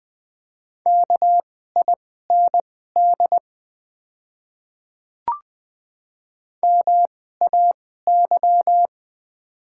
Morse code, heard as a keyed tone, about 20 wpm, spaced as for 15 wpm, 700 Hz. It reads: KIND MAY
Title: 2x Character Spacing - Top 200 Words - 20wpm